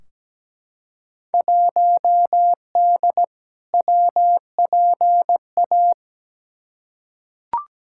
Morse code, heard as a keyed tone, about 17 wpm, 700 Hz. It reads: 1D WPA
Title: Field Day Exchanges - 17wpm